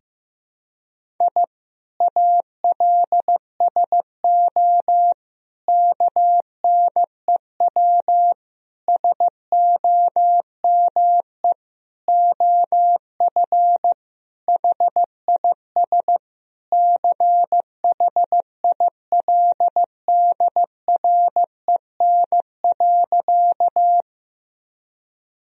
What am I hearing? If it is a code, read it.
Morse code, heard as a keyed tone, about 16 wpm, 700 Hz. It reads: I ALSO KNEW SOME OF HIS CHILDREN.
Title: Sentences from Top 1000 Words - Review Part 1 - 15wpm